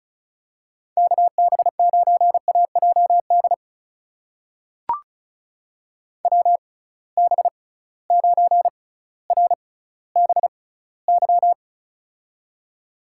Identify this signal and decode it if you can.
Morse code, heard as a keyed tone, about 36 wpm, spaced as for 18 wpm, 700 Hz. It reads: KB9AJD WB9RBY
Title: USA Call Signs - 6 Characters - 6x Character Spacing - 35wpm